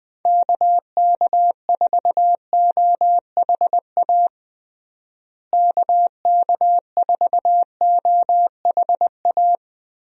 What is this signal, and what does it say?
Morse code, heard as a keyed tone, about 21 wpm, 700 Hz. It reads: KK4OHA KK4OHA
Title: USA Call Signs - 6 Characters - Repeated 3x - 20wpm